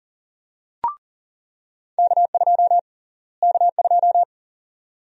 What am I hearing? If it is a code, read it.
Morse code, heard as a keyed tone, about 40 wpm, 700 Hz. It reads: K2 K2
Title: USA Call Sign Prefixes - Repeated 2x - 40wpm